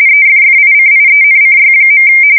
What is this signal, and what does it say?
RTTY, about 45.45 baud. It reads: EFFICIENCY.
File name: fsk170 / teletype-RTTY